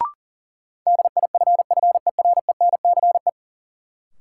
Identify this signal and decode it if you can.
Morse code, heard as a keyed tone, about 40 wpm, 700 Hz. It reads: DIFFERENCE
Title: Sentences from Top 800 Words - New Word Drill - 40wpm